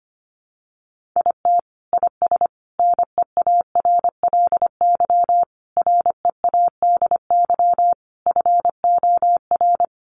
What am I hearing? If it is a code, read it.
Morse code, heard as a keyed tone, about 25 wpm, 700 Hz. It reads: IT IS NEARLY READY FOR
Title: Sentences from Top 1500 Words 25wpm